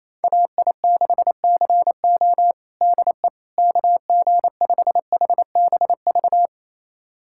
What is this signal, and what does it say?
Morse code, heard as a keyed tone, about 28 wpm, 700 Hz. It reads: AI6CO DE KG5HBV
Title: USA Callsign DE Callsign 28wpm